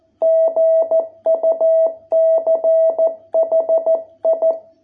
Morse code, heard as a keyed tone, about 14 wpm, 660 Hz. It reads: GUCHI